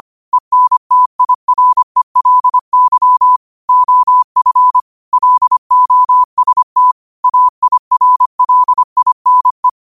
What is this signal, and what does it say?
Morse code, heard as a keyed tone, about 25 wpm, 1000 Hz. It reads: ENTIRELY OF LOST AIRLINE
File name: 25wpm